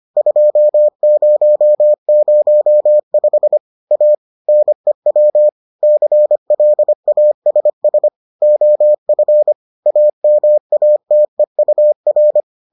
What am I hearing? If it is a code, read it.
Morse code, heard as a keyed tone, about 25 wpm, 600 Hz. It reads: 2005 A NEW CLASS OF AMATEUR